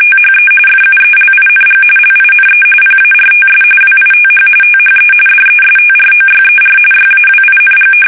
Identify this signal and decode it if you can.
RTTY, about 45.45 baud. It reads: THE FUELING LINE PARTS WITH THE DESTROYER YAWING
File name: fsk850 / battleshipnj